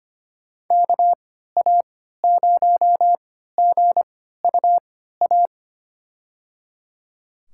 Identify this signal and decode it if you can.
Morse code, heard as a keyed tone, about 25 wpm, spaced as for 14 wpm, 700 Hz. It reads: KA0GUA
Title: USA Call Signs - 6 Characters - 3x Character Spacing - 25wpm